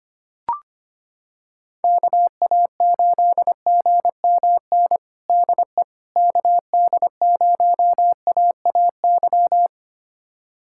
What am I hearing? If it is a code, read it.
Morse code, heard as a keyed tone, about 25 wpm, 700 Hz. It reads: KA8GMN DE KD0AAY
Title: USA Callsign DE Callsign 25wpm